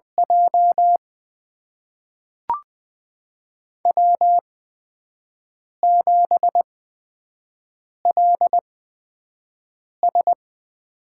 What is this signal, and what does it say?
Morse code, heard as a keyed tone, about 20 wpm, spaced as for 5 wpm, 700 Hz. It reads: J W7LS
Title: USA Call Signs - 4 Characters - 8x Character Spacing - 20wpm